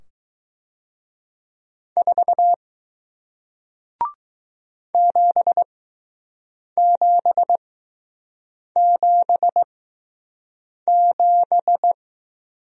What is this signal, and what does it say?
Morse code, heard as a keyed tone, about 20 wpm, spaced as for 6 wpm, 700 Hz. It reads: 4 7777